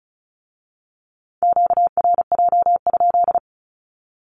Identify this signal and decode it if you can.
Morse code, heard as a keyed tone, about 36 wpm, 700 Hz. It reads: QRJ?
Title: Q-Codes with Spoken Meaning - 35wpm